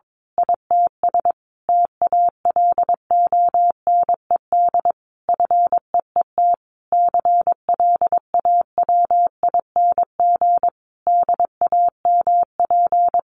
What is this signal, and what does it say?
Morse code, heard as a keyed tone, about 22 wpm, 700 Hz. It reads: ITS TALONED FEET CLAWING DAMP
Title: A Sound Of Thunder 22wpm